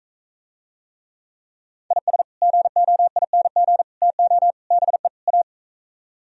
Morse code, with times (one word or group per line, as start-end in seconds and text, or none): 1.9-2.2 IS
2.4-3.8 GOING
4.0-4.5 TO
4.7-5.1 BE
5.3-5.4 A